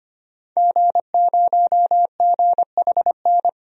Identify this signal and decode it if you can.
Morse code, heard as a keyed tone, about 25 wpm, 700 Hz. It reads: G0GHN
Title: DXCC-100 Call Signs - 5 Characters - 25wpm